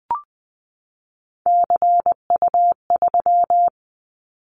Morse code, heard as a keyed tone, about 20 wpm, 700 Hz. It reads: CU3
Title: Callsign Prefixes for Top 50 DXCC - 20wpm